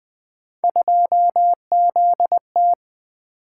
Morse code, heard as a keyed tone, about 20 wpm, 700 Hz. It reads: 2ZT
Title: Three Letter-Number Combinations - 20wpm